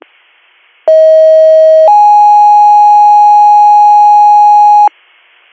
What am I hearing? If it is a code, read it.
Two-tone page, heard as a one-second tone, then a three-second tone, about 640 then 830 Hz. It is heard from 0.9 to 4.9 s.